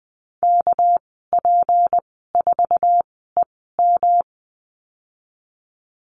Morse code, heard as a keyed tone, about 20 wpm, spaced as for 15 wpm, 700 Hz. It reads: KP4EM
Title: 200 Lesson 32g - Letter k - Call Signs 20wpm